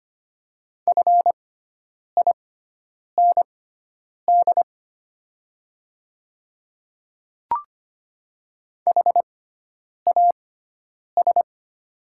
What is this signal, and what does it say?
Morse code, heard as a keyed tone, about 25 wpm, spaced as for 9 wpm, 700 Hz. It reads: FIND HAS
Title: 6x Character Spacing - Top 100 Words - 25wpm